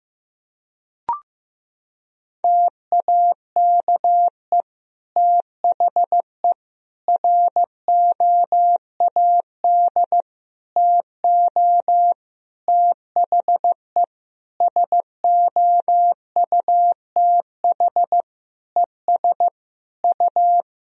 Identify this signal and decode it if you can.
Morse code, heard as a keyed tone, about 15 wpm, 700 Hz. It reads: TAKE THE ROAD TO THE SOUTH ES U
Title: Sentences from Top 400 Words and Abbreviations 15wpm